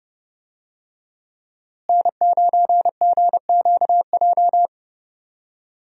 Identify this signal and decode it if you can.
Morse code, heard as a keyed tone, about 30 wpm, 700 Hz. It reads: N9GQJ
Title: USA Call Signs - 5 Characters - Repeated 3x - 30wpm